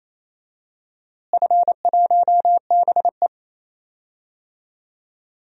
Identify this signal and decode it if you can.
Morse code, heard as a keyed tone, about 27 wpm, 700 Hz. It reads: F1BE
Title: DXCC-100 Call Signs - 4 Characters - Repeated 2x - 28wpm